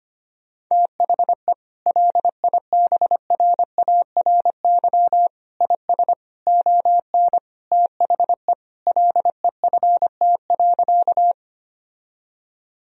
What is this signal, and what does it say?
Morse code, heard as a keyed tone, about 25 wpm, 700 Hz. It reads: THE LIBRARY IS ON THE LEFT.
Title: Sentences from Top 1000 Words - Review Part 4 - 25wpm